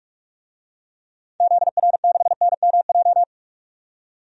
Morse code, heard as a keyed tone, about 46 wpm, 700 Hz. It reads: ZR6NMJ
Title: DXCC-100 Call Signs - 6 Characters - 45wpm